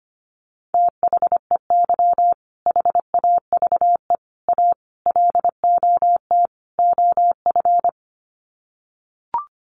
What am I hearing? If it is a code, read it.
Morse code, heard as a keyed tone, about 25 wpm, 700 Hz. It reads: THEY HAVE A LOT OF
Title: Sets of 5 Words Encore2 25wpm